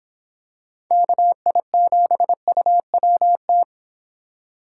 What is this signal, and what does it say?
Morse code, heard as a keyed tone, about 25 wpm, 700 Hz. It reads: KI7UWT